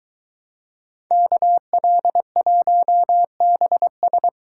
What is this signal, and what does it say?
Morse code, heard as a keyed tone, about 21 wpm, 700 Hz. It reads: KL1BS